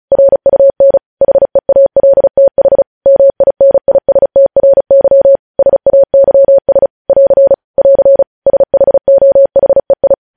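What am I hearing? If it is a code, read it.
Morse code, heard as a keyed tone, about 36 wpm, 550 Hz. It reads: RUN HEALTH MINISTRY SAYS <AR> <AR> SHOHEI
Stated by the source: News headlines sent in Morse at speed 35.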